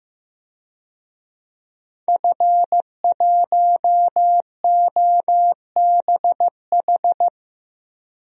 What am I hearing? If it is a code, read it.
Morse code, heard as a keyed tone, about 15 wpm, 700 Hz. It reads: F1OBH
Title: DXCC-100 Call Signs - 5 Characters - 15wpm